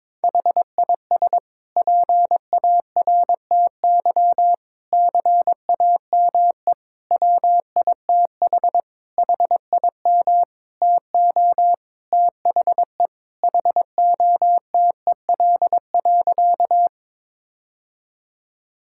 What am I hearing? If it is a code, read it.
Morse code, heard as a keyed tone, about 22 wpm, 700 Hz. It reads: HIS PARTY CAME WITH HIM TO THE HOTEL.
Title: Sentences from Top 1600 Words 22wpm